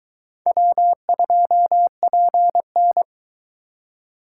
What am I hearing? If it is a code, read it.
Morse code, heard as a keyed tone, about 23 wpm, 700 Hz. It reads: W2PN